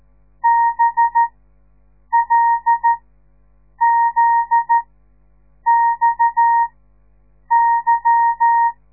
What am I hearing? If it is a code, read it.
Morse code, heard as a keyed tone, about 13 wpm, spaced as for 7 wpm, 930 Hz. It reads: BLZXY
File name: radiomorningstar / morse / les 9